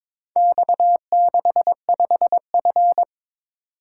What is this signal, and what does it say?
Morse code, heard as a keyed tone, about 22 wpm, 700 Hz. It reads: X65F
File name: Four Letter-Number Combinations - 22wpm